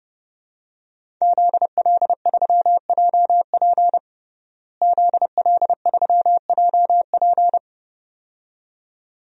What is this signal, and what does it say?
Morse code, heard as a keyed tone, about 30 wpm, 700 Hz. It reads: ZL3JP ZL3JP
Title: DXCC-100 Call Signs - 5 Characters - Repeated 2x - 30wpm